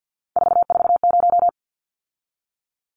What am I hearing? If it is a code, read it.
Morse code, heard as a keyed tone, about 50 wpm, 700 Hz. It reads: 440
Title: Numbers - 3-Digit 50wpm